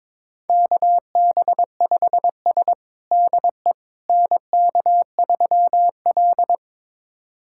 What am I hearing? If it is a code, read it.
Morse code, heard as a keyed tone, about 22 wpm, 700 Hz. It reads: KB5S DE NK3L
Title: USA Callsign DE Callsign - 4 Characters - 22wpm